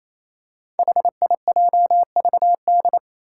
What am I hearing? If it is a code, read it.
Morse code, heard as a keyed tone, about 27 wpm, 700 Hz. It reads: HIJVD
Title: Five Challenging Letter-Number Combinations 28wpm